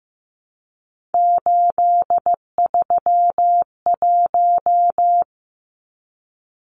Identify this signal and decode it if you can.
Morse code, heard as a keyed tone, about 15 wpm, 700 Hz. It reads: 831
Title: Numbers - 3-Digit 15wpm